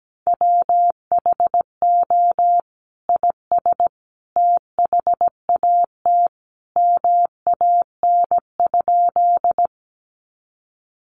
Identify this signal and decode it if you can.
Morse code, heard as a keyed tone, about 17 wpm, 700 Hz. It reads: WHO IS THAT MAN?
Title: Sentences from Top 200 Words 17wpm